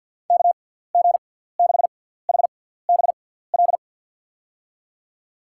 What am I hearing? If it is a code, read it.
Morse code, heard as a keyed tone, about 49 wpm, spaced as for 17 wpm, 700 Hz. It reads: KG6HBL
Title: USA Call Signs - 6x Character Spacing - 50wpm